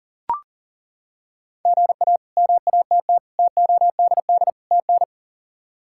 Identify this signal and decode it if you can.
Morse code, heard as a keyed tone, about 37 wpm, 700 Hz. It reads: GA MATT TODD TN
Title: K1USN SST QSO Practice - 40wpm